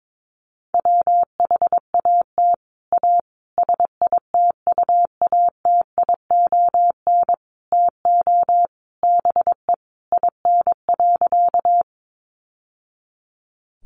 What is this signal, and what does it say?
Morse code, heard as a keyed tone, about 22 wpm, 700 Hz. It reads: WHAT A SITUATION TO BE IN.
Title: Sentences from Top 1200 Words 22wpm